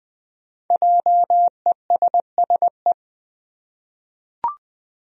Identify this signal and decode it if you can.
Morse code, heard as a keyed tone, about 20 wpm, 700 Hz. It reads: JESSE
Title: USA Common Names - Max 5 Letters - 20wpm